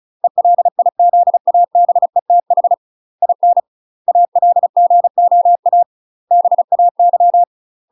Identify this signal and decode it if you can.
Morse code, heard as a keyed tone, about 34 wpm, 700 Hz. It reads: ELIZABETH IN ALGOA BAY
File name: UK MASTER QSOs TRAINING / 35WPM / UK MASTER QSOs 0019